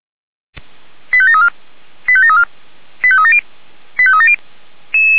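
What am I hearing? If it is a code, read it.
Five-tone selective call (ZVEI), heard as quick runs of five tones, tones of about 70 ms. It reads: 85734 85734 85370 85370
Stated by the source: ZVEI five-tone call received by SDR.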